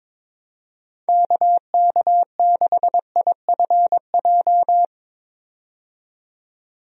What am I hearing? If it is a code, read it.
Morse code, heard as a keyed tone, about 22 wpm, 700 Hz. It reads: KK6IFJ
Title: USA Call Signs - 6 Characters - 22wpm